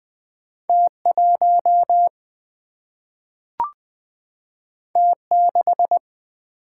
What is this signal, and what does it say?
Morse code, heard as a keyed tone, about 20 wpm, 700 Hz. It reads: T1 T6
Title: Single-Letter Single-Number 20wpm